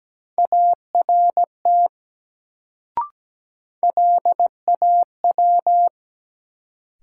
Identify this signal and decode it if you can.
Morse code, heard as a keyed tone, about 17 wpm, 700 Hz. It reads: ART LAW
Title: Sentences from Top 600 Words - New Word Drill - 17wpm